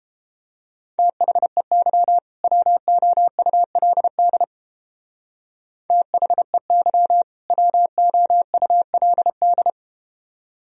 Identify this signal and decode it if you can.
Morse code, heard as a keyed tone, about 30 wpm, 700 Hz. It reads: THEY WOULD THEY WOULD